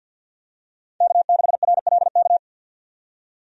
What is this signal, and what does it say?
Morse code, heard as a keyed tone, about 52 wpm, 700 Hz. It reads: K6RLK
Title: USA Call Signs - 5 Characters - 50wpm